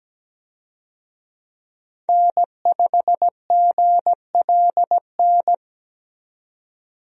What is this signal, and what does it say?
Morse code, heard as a keyed tone, about 17 wpm, 700 Hz. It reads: N5GLN
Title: USA Call Signs - 5 Characters - 17wpm